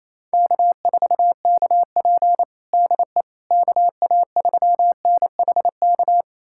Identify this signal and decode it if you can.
Morse code, heard as a keyed tone, about 28 wpm, 700 Hz. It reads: K4KP DE KA3NHK
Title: USA Callsign DE Callsign 28wpm